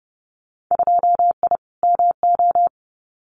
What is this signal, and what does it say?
Morse code, heard as a keyed tone, about 30 wpm, 700 Hz. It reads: 2I MO
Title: Winter Field Day Exchanges 30wpm